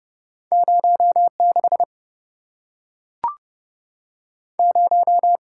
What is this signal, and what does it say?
Morse code, heard as a keyed tone, about 26 wpm, 700 Hz. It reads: 06 0